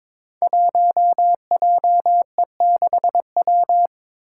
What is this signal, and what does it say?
Morse code, heard as a keyed tone, about 21 wpm, 700 Hz. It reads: 1JE6W
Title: Five Challenging Letter-Number Combinations 22wpm